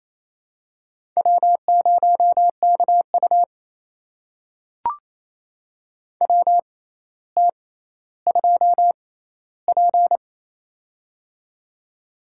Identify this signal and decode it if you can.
Morse code, heard as a keyed tone, about 28 wpm, spaced as for 14 wpm, 700 Hz. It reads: W0KU WT2P
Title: USA Call Signs - 4 Characters - 6x Character Spacing - 28wpm